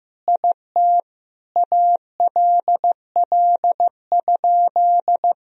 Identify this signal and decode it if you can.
Morse code, heard as a keyed tone, about 15 wpm, 700 Hz. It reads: IT ALL?
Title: Sentences from Top 1000 Words - Review Part 2 - 15wpm